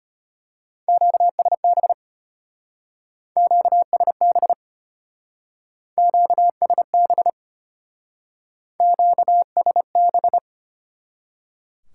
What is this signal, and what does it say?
Morse code, heard as a keyed tone, about 31 wpm, 700 Hz. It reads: QSB QSB QSB QSB
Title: Speed-Racing Top 90 QSO Elements 25wpm